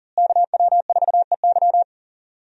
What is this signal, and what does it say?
Morse code, heard as a keyed tone, about 40 wpm, 690 Hz. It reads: KW4EY
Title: USA Call Signs - 5 Characters - Repeated 3x - 40wpm